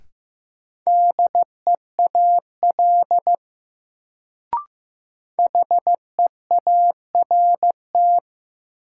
Morse code, heard as a keyed tone, about 15 wpm, 700 Hz. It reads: DEAL HEART
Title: Sets of 3 Words - New Word Drill - 15wpm